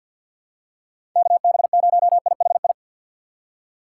Morse code, heard as a keyed tone, about 51 wpm, 700 Hz. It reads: KB0IHI
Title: USA Call Signs - 6 Characters - 4x Character Spacing - 50wpm